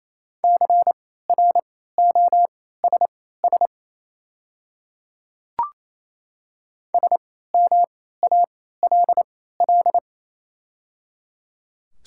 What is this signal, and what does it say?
Morse code, heard as a keyed tone, about 27 wpm, spaced as for 16 wpm, 700 Hz. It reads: CROSS SMALL
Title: 3x Character Spacing - Top 300 Words - 28wpm